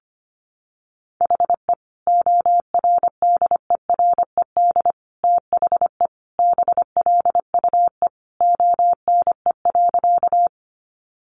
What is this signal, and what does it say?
Morse code, heard as a keyed tone, about 25 wpm, 700 Hz. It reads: HE ORDERED THE BLUE ONE.
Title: Additional Easy Phrases 25wpm